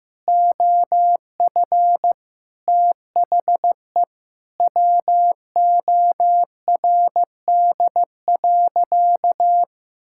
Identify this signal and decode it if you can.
Morse code, heard as a keyed tone, about 16 wpm, 700 Hz. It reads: OF THE WORD.
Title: Sentences from Top 1000 Words - Review Part 6 - 15wpm